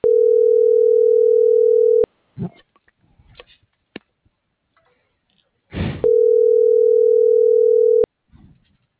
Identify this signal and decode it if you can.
Ringback tone (North America), 440 + 480 Hz.